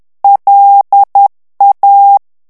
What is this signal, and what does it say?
Morse code, heard as a keyed tone, about 10 wpm, 800 Hz. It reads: LA
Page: Français - texte en clair 1 - vitesse : 10 mots minute